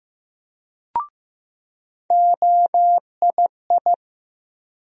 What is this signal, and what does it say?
Morse code, heard as a keyed tone, about 15 wpm, 700 Hz. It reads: OII
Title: USA Call Sign Postfixes - 3 Letters 15wpm